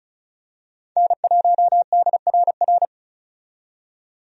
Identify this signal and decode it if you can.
Morse code, heard as a keyed tone, about 36 wpm, 700 Hz. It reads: N1DRR
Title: USA Call Signs - 5 Characters - 35wpm